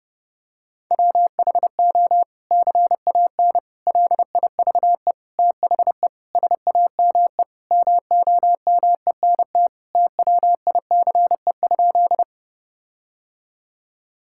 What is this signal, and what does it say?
Morse code, heard as a keyed tone, about 30 wpm, 700 Hz. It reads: WHO CAN LIVE THE SAME MOMENT TWICE?
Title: Sentences from Top 1600 Words 30wpm